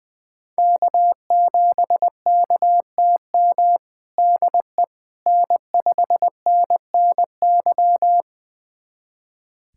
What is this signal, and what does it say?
Morse code, heard as a keyed tone, about 20 wpm, 700 Hz. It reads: K7KTM DE N5NNY
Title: USA Callsign DE Callsign 20wpm